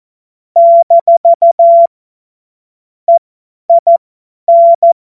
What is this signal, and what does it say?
Morse code, heard as a keyed tone, about 14 wpm, spaced as for 10 wpm, 680 Hz. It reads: - EIN
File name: Morsekurs / Witze / Witz 03. August 2021, 50 F70